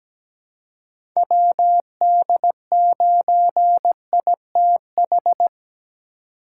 Morse code, heard as a keyed tone, about 17 wpm, 700 Hz. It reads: WD9ITH
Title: USA Call Signs - 6 Characters - 2x Character Spacing - 17wpm